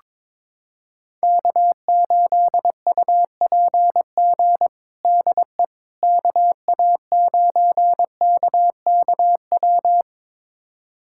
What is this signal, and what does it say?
Morse code, heard as a keyed tone, about 22 wpm, 700 Hz. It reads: K8UPG DE KA9KKW
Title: USA Callsign DE Callsign 22wpm